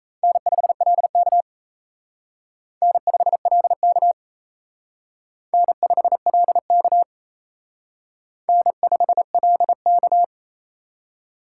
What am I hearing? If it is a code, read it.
Morse code, heard as a keyed tone, about 35 wpm, 700 Hz. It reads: N5LK N5LK N5LK N5LK
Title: Speed Racing - USA 1x2 Call Signs 28wpm